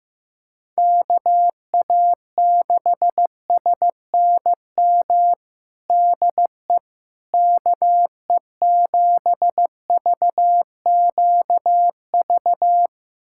Morse code, heard as a keyed tone, about 15 wpm, 700 Hz. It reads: KA6SNM DE KE7VQV
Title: USA Callsign DE Callsign - 6 Characters - 15wpm